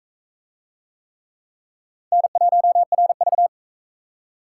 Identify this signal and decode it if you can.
Morse code, heard as a keyed tone, about 36 wpm, 700 Hz. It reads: N1RV